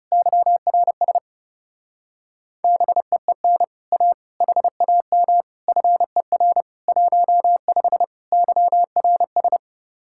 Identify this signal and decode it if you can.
Morse code, heard as a keyed tone, about 31 wpm, 700 Hz. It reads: YRS BEEN A HAM FER 15 YRS
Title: Speed-Racing Common QSO Phrases - 30wpm